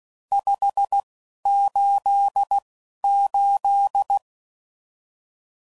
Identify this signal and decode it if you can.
Morse code, heard as a keyed tone, about 16 wpm, spaced as for 11 wpm, 800 Hz. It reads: 588
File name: CW Code Groups @10WPM / RSTs / Various RST No's